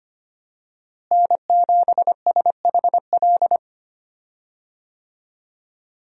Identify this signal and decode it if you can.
Morse code, heard as a keyed tone, about 25 wpm, 700 Hz. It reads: N7SHL
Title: USA Call Signs - 5 Characters - Repeated 2x - 25wpm